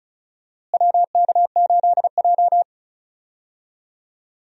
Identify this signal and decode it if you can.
Morse code, heard as a keyed tone, about 36 wpm, 700 Hz. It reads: WK8J
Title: USA Call Signs - 4 Characters - Repeated 2x - 35wpm